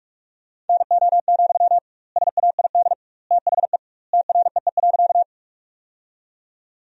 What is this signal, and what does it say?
Morse code, heard as a keyed tone, about 43 wpm, 700 Hz. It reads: NO, SAID THE TREE.
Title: 2x Word-Spacing - Sentences from Top 300 Words - 45wpm